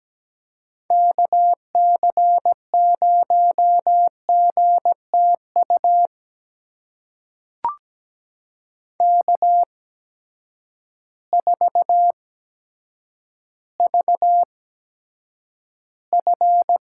Morse code, heard as a keyed tone, about 17 wpm, spaced as for 8 wpm, 700 Hz. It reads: KC0GTU K4VF
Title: USA Call Signs - 8x Character Spacing - 17wpm